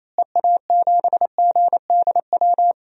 Morse code, heard as a keyed tone, about 28 wpm, 700 Hz. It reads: EA7GDW
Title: DXCC-100 Call Signs - 6 Characters - Repeated 2x - 28wpm